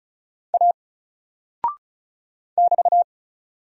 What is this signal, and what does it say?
Morse code, heard as a keyed tone, about 36 wpm, 700 Hz. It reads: A <BT>
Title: Single Character (Letter:Number:Punctuation:Prosign) 35wpm